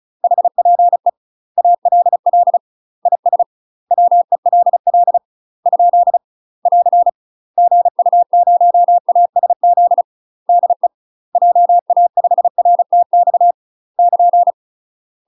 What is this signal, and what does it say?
Morse code, heard as a keyed tone, about 36 wpm, 700 Hz. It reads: HPE ALL IS WELL ? <AR> GU0ASZ DE JA5RTX <KN>
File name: UK MASTER QSOs TRAINING / 35WPM / UK MASTER QSOs 0041